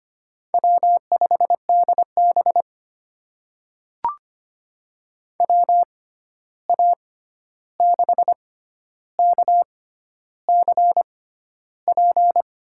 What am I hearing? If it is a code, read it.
Morse code, heard as a keyed tone, about 25 wpm, spaced as for 11 wpm, 700 Hz. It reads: W5DB WA6KCP
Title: USA Call Signs - 6x Character Spacing - 25wpm